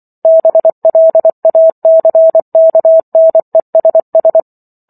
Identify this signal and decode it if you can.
Morse code, heard as a keyed tone, about 24 wpm, 650 Hz. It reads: BLACKNESS
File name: Chapter 15 - 24 WPM